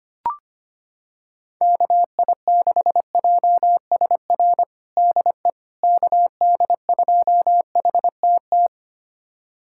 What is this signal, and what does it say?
Morse code, heard as a keyed tone, about 25 wpm, 700 Hz. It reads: KI6JSR DE KD2HTT
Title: USA Callsign DE Callsign - 6 Characters - 25wpm